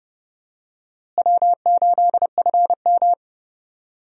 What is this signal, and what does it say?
Morse code, heard as a keyed tone, about 31 wpm, 700 Hz. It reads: W8FM
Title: USA Call Signs - 4 Characters - Repeated 2x - 30wpm